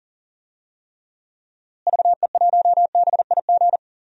Morse code, heard as a keyed tone, about 37 wpm, 700 Hz. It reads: VE1BIG
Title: DXCC-100 Call Signs - 6 Characters - Repeated 3x - 40wpm